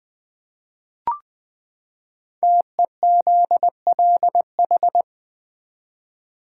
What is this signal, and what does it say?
Morse code, heard as a keyed tone, about 20 wpm, 700 Hz. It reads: TEZLH
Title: Code Groups - 5 Letters - 20wpm